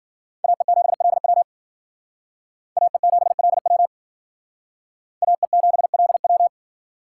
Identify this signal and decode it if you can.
Morse code, heard as a keyed tone, about 52 wpm, 700 Hz. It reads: AE7LW AE7LW AE7LW